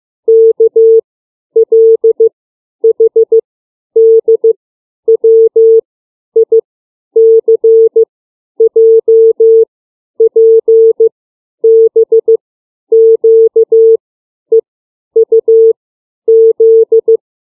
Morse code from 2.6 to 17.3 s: H D W I C J P B Q E U Z